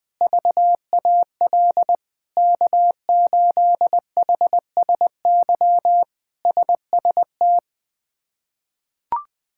Morse code, heard as a keyed tone, about 20 wpm, 700 Hz. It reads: VAL K8HSY SST
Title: K1USN SST QSO Practice - 20wpm